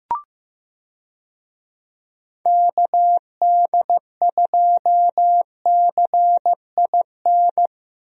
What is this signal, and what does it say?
Morse code, heard as a keyed tone, about 16 wpm, 700 Hz. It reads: KD2CIN
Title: USA Call Signs - 6 Characters - Repeated 3x - 15wpm